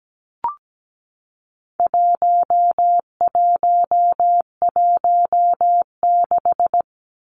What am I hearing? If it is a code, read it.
Morse code, heard as a keyed tone, about 17 wpm, 700 Hz. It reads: 1116
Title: Numbers - 4-Digit 17wpm